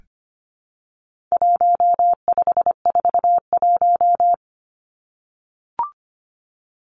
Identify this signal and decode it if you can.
Morse code, heard as a keyed tone, about 25 wpm, 700 Hz. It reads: 1541
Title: Numbers - 4-Digit 25wpm